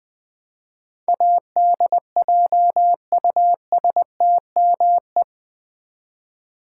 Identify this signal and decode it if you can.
Morse code, heard as a keyed tone, about 20 wpm, 700 Hz. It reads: ADJUSTME
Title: Build A Word - 4 Letters At A Time - 10-Letter Words - 20wpm